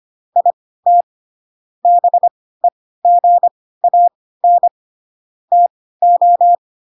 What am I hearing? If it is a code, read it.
Morse code, heard as a keyed tone, about 25 wpm, spaced as for 16 wpm, 700 Hz. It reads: IT BEGAN TO
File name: Pinocchio / Pinocchio ch3 10 25 ik1hge